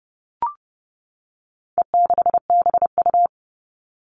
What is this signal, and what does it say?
Morse code, heard as a keyed tone, about 30 wpm, 700 Hz. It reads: E6BU
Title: All Cut Numbers - 4-Digit 30wpm